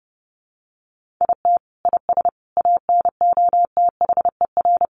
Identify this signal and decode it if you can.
Morse code, heard as a keyed tone, about 28 wpm, 700 Hz. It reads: IT IS ANOTHER
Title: Sentences from Top 1400 Words 30wpm